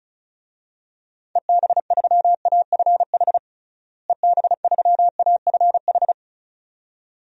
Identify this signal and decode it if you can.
Morse code, heard as a keyed tone, about 34 wpm, 700 Hz. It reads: EB3AFH EB3AFH
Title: DXCC-100 Call Signs - 6 Characters - Repeated 2x - 35wpm